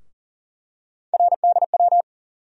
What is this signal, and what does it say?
Morse code, heard as a keyed tone, about 40 wpm, 700 Hz. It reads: RDW
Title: USA Call Sign Postfixes - 3 Letters 40wpm